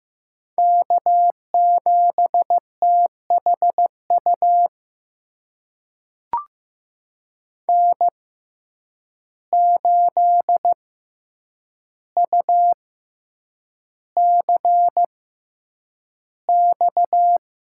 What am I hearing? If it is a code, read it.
Morse code, heard as a keyed tone, about 15 wpm, spaced as for 8 wpm, 700 Hz. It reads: K7THU N8UCX
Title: USA Call Signs - 5 Characters - 6x Character Spacing - 15wpm